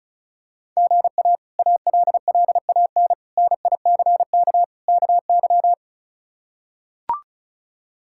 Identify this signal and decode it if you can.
Morse code, heard as a keyed tone, about 36 wpm, 700 Hz. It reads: GA ALLAN NICK KY
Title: K1USN SST QSO Practice - 35wpm